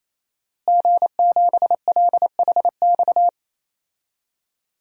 Morse code, heard as a keyed tone, about 28 wpm, 700 Hz. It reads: G7LHX
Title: DXCC-100 Call Signs - 5 Characters - Repeated 3x - 28wpm